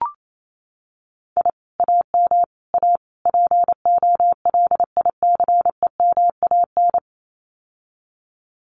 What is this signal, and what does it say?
Morse code, heard as a keyed tone, about 28 wpm, 700 Hz. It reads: I AM A POLICEMAN
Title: Common QSO Phrases Encore 28wpm